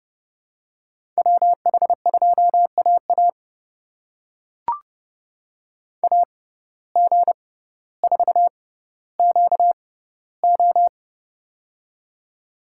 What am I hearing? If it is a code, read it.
Morse code, heard as a keyed tone, about 30 wpm, spaced as for 15 wpm, 700 Hz. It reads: WH2AA AG4QO
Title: USA Call Signs - 5 Characters - 6x Character Spacing - 30wpm